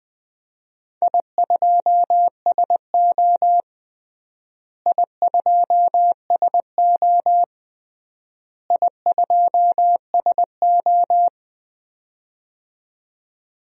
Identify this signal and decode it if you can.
Morse code, heard as a keyed tone, about 21 wpm, 700 Hz. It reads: I2SO I2SO I2SO
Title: DXCC-100 Call Signs - 4 Characters - Repeated 3x - 20wpm